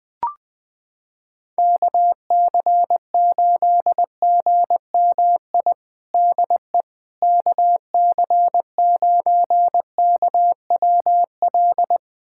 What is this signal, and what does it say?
Morse code, heard as a keyed tone, about 20 wpm, 700 Hz. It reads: KC8GMI DE KC9KWL
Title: USA Callsign DE Callsign - 6 Characters - 20wpm